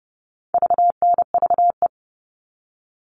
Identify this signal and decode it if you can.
Morse code, heard as a keyed tone, about 30 wpm, 700 Hz. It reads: VNVE
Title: All Cut Numbers - 4-Digit 30wpm